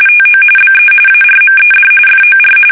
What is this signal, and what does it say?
RTTY, about 45.45 baud. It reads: TO BE ARMED FOR
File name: fsk850 / usnavy1